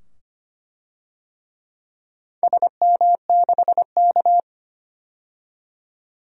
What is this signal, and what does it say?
Morse code, heard as a keyed tone, about 25 wpm, 700 Hz. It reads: SM6K
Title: DXCC-100 Call Signs - 4 Characters - 25wpm